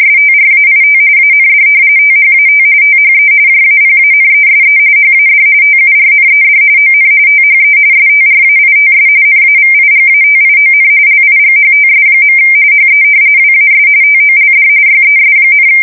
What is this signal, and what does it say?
RTTY, about 45.45 baud. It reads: STRAIT AND THAT THE JAPANESE HAD TURNED ON THE STRAITS NAVIGATION LIGHTS. HE WAS TOLD YES,
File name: fsk170 / battleshipnj